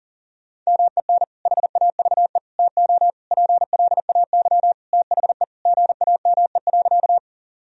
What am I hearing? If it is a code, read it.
Morse code, heard as a keyed tone, about 40 wpm, 700 Hz. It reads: MEN HAVE TO PLAY THE GAME.
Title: Sentences from Top 1000 Words - Review Part 3 - 40wpm